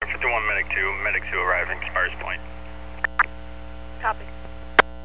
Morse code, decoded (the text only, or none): none